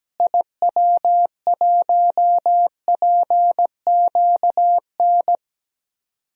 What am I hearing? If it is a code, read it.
Morse code, heard as a keyed tone, about 17 wpm, 700 Hz. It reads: IW1PQN
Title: DXCC-100 Call Signs - 6 Characters - Repeated 2x - 17wpm